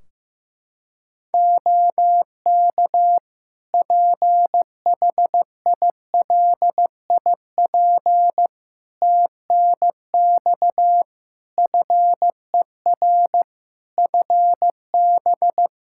Morse code, heard as a keyed tone, about 15 wpm, 700 Hz. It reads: OK PHILIP TNX FER FB
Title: K1USN SST QSO Practice 15wpm